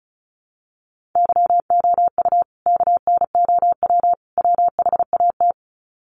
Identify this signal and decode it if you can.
Morse code, heard as a keyed tone, about 34 wpm, 700 Hz. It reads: YOU KNOW WHAT
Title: Sets of 3 Words 35wpm